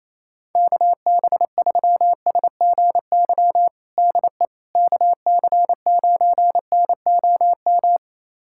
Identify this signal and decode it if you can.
Morse code, heard as a keyed tone, about 28 wpm, 700 Hz. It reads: KB3SGY DE KC9NOM
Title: USA Callsign DE Callsign - 6 Characters - 28wpm